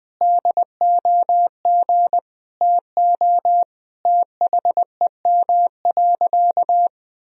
Morse code, heard as a keyed tone, about 20 wpm, 700 Hz. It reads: DOG TO THEM.
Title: Sentences from Top 1000 Words - Review Part 2 - 20wpm